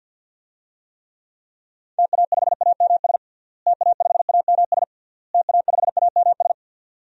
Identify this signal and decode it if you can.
Morse code, heard as a keyed tone, about 49 wpm, 700 Hz. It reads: TA5AMS TA5AMS TA5AMS
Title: DXCC-100 Call Signs - 6 Characters - Repeated 3x - 50wpm